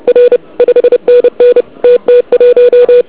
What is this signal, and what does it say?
Morse code, heard as a keyed tone, about 29 wpm, 1470 Hz. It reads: R 5NN TT1
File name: ARCHIV / KX3 / KV PA